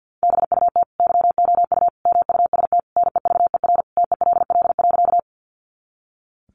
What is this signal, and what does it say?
Morse code, heard as a keyed tone, about 50 wpm, 700 Hz. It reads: BUT YOU MUST NEVER TELL.
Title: Sentences from Top 300 Words 50wpm